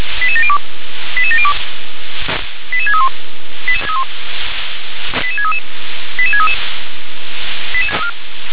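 Five-tone selective call (ZVEI), heard as quick runs of five tones, tones of about 70 ms. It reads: 88702 88702 88631 88633 88633
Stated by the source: ZVEI five-tone selective call received by SDR.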